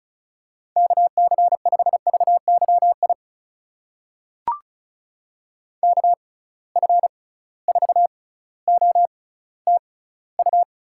Morse code, heard as a keyed tone, about 36 wpm, spaced as for 18 wpm, 700 Hz. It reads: KC5VYI KF4OTU
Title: USA Call Signs - 6 Characters - 6x Character Spacing - 35wpm